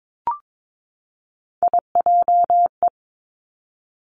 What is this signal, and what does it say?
Morse code, heard as a keyed tone, about 22 wpm, 700 Hz. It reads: IJE
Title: Three Challenging Letter-Number Combinations 22wpm